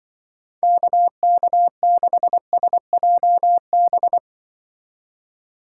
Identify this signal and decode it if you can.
Morse code, heard as a keyed tone, about 23 wpm, 700 Hz. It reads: KK6SJB
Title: Speed Racing - USA 2x3 Call Signs 20wpm